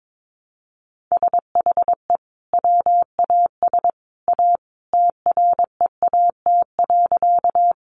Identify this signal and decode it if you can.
Morse code, heard as a keyed tone, about 22 wpm, 700 Hz. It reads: SHE WAS A TREAT.
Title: Sentences from Top 1300 Words 22wpm